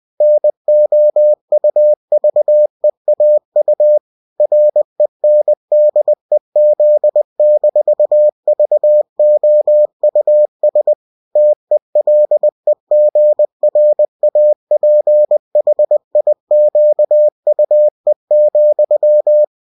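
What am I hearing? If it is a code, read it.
Morse code, heard as a keyed tone, about 20 wpm, 600 Hz. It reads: NOUVEAU RENDEZ-VOUS TELEGRAPHIQUE,
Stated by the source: UFT bulletin sent in CW.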